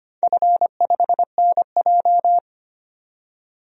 Morse code, heard as a keyed tone, about 25 wpm, 700 Hz. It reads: F5NJ
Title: DXCC-100 Call Signs - 4 Characters - 25wpm